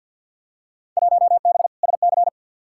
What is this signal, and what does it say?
Morse code, heard as a keyed tone, about 49 wpm, 700 Hz. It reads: 1B SC
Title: Field Day Exchanges Encore - 50wpm